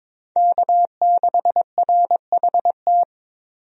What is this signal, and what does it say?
Morse code, heard as a keyed tone, about 22 wpm, 700 Hz. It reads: K6RHT
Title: USA Call Signs - 5 Characters - 22wpm